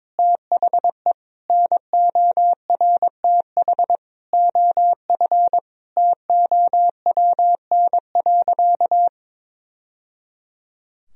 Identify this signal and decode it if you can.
Morse code, heard as a keyed tone, about 22 wpm, 700 Hz. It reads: THE NORTH OF TOWN.
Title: Sentences from Top 1000 Words - Review Part 4 - 22wpm